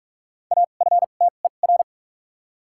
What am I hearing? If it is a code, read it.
Morse code, heard as a keyed tone, about 41 wpm, spaced as for 31 wpm, 700 Hz. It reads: AFTER
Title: 2x Character Spacing - Top 200 Words - 45wpm